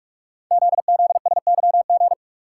Morse code, heard as a keyed tone, about 44 wpm, 700 Hz. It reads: ZZSYG
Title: Code Groups - 5 Letters - 45wpm